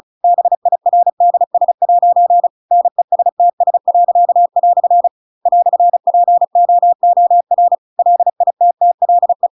Morse code, heard as a keyed tone, about 34 wpm, 700 Hz. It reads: BIRDS' NESTS." "POOR LITTLE